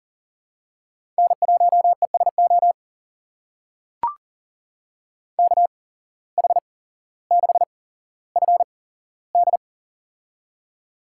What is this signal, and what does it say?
Morse code, heard as a keyed tone, about 40 wpm, spaced as for 17 wpm, 700 Hz. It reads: N1ESO KH6FD
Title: USA Call Signs - 5 Characters - 8x Character Spacing - 40wpm